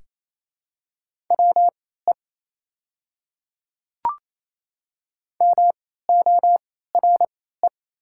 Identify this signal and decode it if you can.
Morse code, heard as a keyed tone, about 27 wpm, spaced as for 16 wpm, 690 Hz. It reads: WE MORE
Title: 3x Character Spacing - Top 100 Words - 28wpm